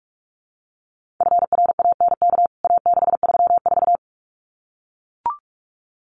Morse code, heard as a keyed tone, about 44 wpm, 700 Hz. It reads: FRANK A634
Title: CWT Exchange 45wpm